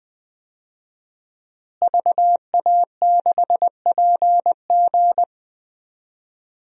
Morse code, heard as a keyed tone, about 21 wpm, 700 Hz. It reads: VA6PG
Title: DXCC-100 Call Signs - 5 Characters - Repeated 2x - 20wpm